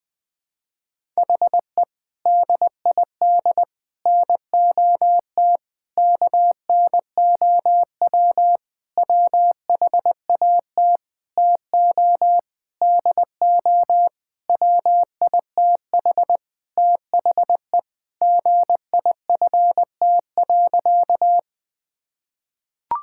Morse code, heard as a keyed tone, about 20 wpm, 700 Hz. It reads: HE DID NOT KNOW WHAT TO DO WITH THE GIFT.
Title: Sentences with 5-Word Phrases - 20wpm